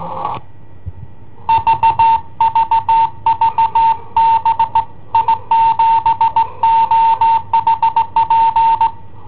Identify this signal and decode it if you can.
Morse code, heard as a keyed tone, about 19 wpm, 910 Hz. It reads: VVV B I7OHP
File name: i7ohp / cassetta i7ohp B